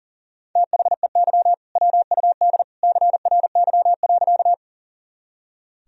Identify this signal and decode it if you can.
Morse code, heard as a keyed tone, about 40 wpm, 700 Hz. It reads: THEY WUD CRY.
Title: Sentences from Top 500 Words and Abbreviations 40wpm